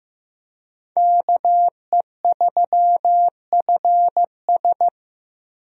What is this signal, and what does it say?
Morse code, heard as a keyed tone, about 15 wpm, 700 Hz. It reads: KE3FS
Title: USA Call Signs - 5 Characters - Repeated 2x - 15wpm